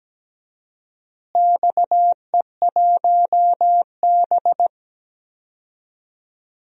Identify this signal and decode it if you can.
Morse code, heard as a keyed tone, about 17 wpm, 700 Hz. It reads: XE1B
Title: DXCC-100 Call Signs - 4 Characters - Repeated 2x - 17wpm